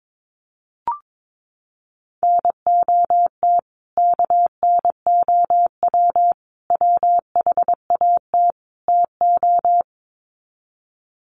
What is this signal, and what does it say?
Morse code, heard as a keyed tone, about 22 wpm, 700 Hz. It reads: NOT KNOW WHAT TO
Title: Sets of 4 Words 22wpm